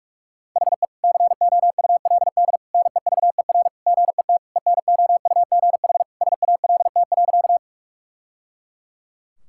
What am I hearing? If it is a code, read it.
Morse code, heard as a keyed tone, about 46 wpm, 700 Hz. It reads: HE COULD NEVER GET ENOUGH SALT.
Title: Sentences from Top 1000 Words - Review Part 5 - 45wpm